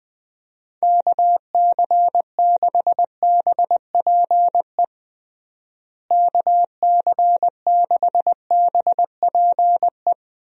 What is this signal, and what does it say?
Morse code, heard as a keyed tone, about 20 wpm, 700 Hz. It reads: KC6BPE KC6BPE
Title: USA Call Signs - 6 Characters - Repeated 2x - 20wpm